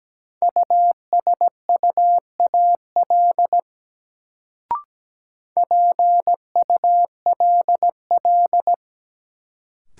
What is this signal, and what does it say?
Morse code, heard as a keyed tone, about 17 wpm, 700 Hz. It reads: USUAL PULL
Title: Sentences from Top 400 Words - New Word Drill - 17wpm